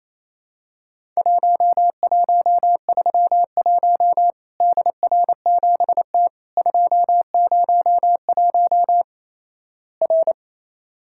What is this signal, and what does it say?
Morse code, heard as a keyed tone, about 28 wpm, 700 Hz. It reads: 1131 DR7T 201 R
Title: Worked All Europe DX Contest - QTC Exchange - 28wpm